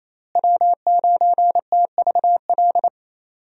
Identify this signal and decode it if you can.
Morse code, heard as a keyed tone, about 28 wpm, 700 Hz. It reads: W9TVL
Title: USA Call Signs - 5 Characters - Repeated 2x - 28wpm